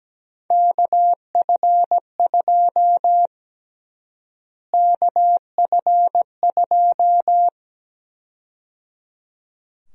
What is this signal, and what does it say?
Morse code, heard as a keyed tone, about 17 wpm, 700 Hz. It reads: KF2 KF2
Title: USA Call Sign Prefixes - Repeated 2x - 17wpm